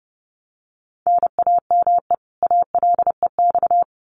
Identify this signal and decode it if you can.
Morse code, heard as a keyed tone, about 30 wpm, 700 Hz. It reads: NAME ALEX
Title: Common QSO Phrases 30wpm